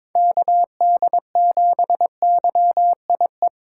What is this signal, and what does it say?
Morse code, heard as a keyed tone, about 21 wpm, 700 Hz. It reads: KD7YIE
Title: USA Call Signs - 2x Character Spacing - 22wpm